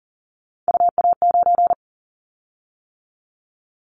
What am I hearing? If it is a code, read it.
Morse code, heard as a keyed tone, about 40 wpm, 700 Hz. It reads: UA9
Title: Callsign Prefixes for Top 25 DXCC - 40wpm